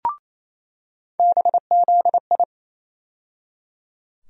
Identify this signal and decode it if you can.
Morse code, heard as a keyed tone, about 28 wpm, 700 Hz. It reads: BZI
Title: USA Call Sign Postfixes - 3 Letters 28wpm